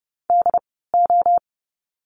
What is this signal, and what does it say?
Morse code, heard as a keyed tone, about 30 wpm, spaced as for 17 wpm, 700 Hz. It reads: DO
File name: course / 30:10wpm / 115 Lesson 19f - Letter w - Words 30wpm